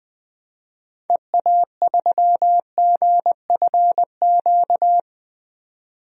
Morse code, heard as a keyed tone, about 20 wpm, 700 Hz. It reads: EA3GFQ
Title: DXCC-100 Call Signs - 6 Characters - 20wpm